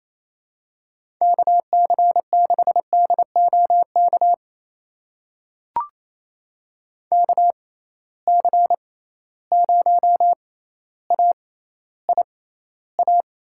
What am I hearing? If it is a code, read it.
Morse code, heard as a keyed tone, about 28 wpm, spaced as for 14 wpm, 700 Hz. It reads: KC6DOK KC0AIA
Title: USA Call Signs - 6 Characters - 6x Character Spacing - 28wpm